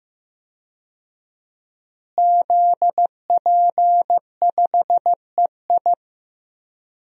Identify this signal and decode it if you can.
Morse code, heard as a keyed tone, about 15 wpm, 700 Hz. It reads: ZP5EI
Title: DXCC-100 Call Signs - 5 Characters - Repeated 2x - 15wpm